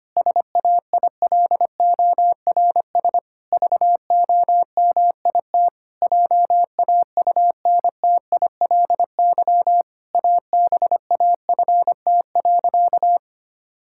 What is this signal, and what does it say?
Morse code, heard as a keyed tone, about 25 wpm, 700 Hz. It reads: SAILORS VOMIT JAUNTILY ABAFT.
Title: Pangram Challenge v2 - 25wpm